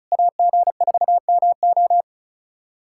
Morse code, heard as a keyed tone, about 36 wpm, 700 Hz. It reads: AG4MO
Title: SOTA QSO Practice 35wpm